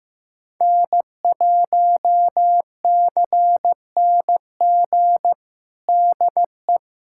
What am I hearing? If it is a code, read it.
Morse code, heard as a keyed tone, about 16 wpm, 700 Hz. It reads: N1CNG DE
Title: USA Callsign DE Callsign - 5 Characters - 15wpm